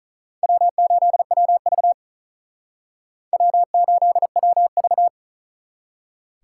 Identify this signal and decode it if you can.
Morse code, heard as a keyed tone, about 46 wpm, 700 Hz. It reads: W8WV W8WV